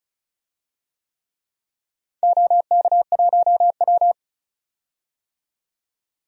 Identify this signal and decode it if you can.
Morse code, heard as a keyed tone, about 36 wpm, 700 Hz. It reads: OK1W
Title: DXCC-100 Call Signs - 4 Characters - 35wpm